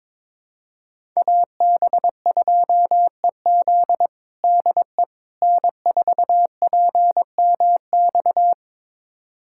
Morse code, heard as a keyed tone, about 22 wpm, 700 Hz. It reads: AB2EZ DE N4PMX
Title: USA Callsign DE Callsign - 5 Characters - 22wpm